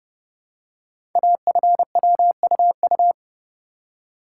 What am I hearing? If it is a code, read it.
Morse code, heard as a keyed tone, about 30 wpm, 700 Hz. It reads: AFWUU
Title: Code Groups - 5 Letters - 30wpm